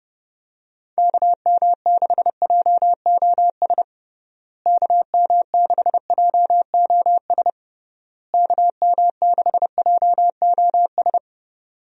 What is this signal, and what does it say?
Morse code, heard as a keyed tone, about 32 wpm, 700 Hz. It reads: KM6JOS KM6JOS KM6JOS
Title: USA Call Signs - 6 Characters - Repeated 3x - 30wpm